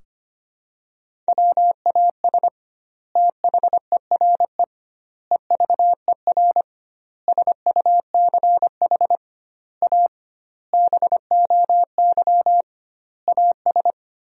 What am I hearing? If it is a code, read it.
Morse code, heard as a keyed tone, about 25 wpm, 700 Hz. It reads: WAS THERE EVER SUCH A BOY AS
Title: 2x Word-Spacing - Sentences from Top 400 Words - 25wpm